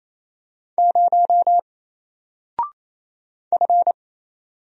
Morse code, heard as a keyed tone, about 28 wpm, 700 Hz. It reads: 0 F
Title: Single Character (Letter:Number:Punctuation:Prosign) Alt 28wpm